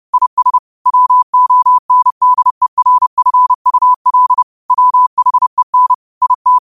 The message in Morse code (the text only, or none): IS WONDERFUL WHEN IT